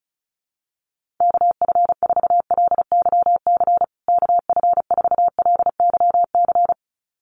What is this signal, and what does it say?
Morse code, heard as a keyed tone, about 34 wpm, 700 Hz. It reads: KF4LYC KF4LYC
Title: USA Call Signs Repeated 2x 35wpm